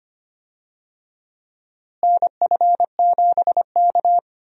Morse code, heard as a keyed tone, about 25 wpm, 700 Hz. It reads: NF7K
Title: USA Call Signs - 4 Characters - Repeated 2x - 25wpm